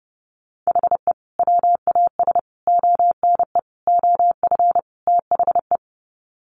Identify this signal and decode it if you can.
Morse code, heard as a keyed tone, about 30 wpm, 700 Hz. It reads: HE WAS ONE OF THE
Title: Sets of 5 Words 30wpm